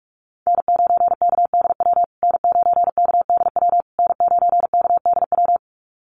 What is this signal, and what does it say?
Morse code, heard as a keyed tone, about 44 wpm, 700 Hz. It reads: N9KDW N9KDW N9KDW
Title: USA Call Signs Repeated 3x 45wpm